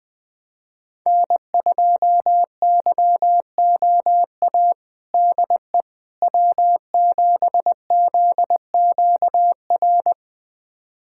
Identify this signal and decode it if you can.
Morse code, heard as a keyed tone, about 21 wpm, 700 Hz. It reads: N2YOA DE W7ZQR
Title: USA Callsign DE Callsign - 5 Characters - 20wpm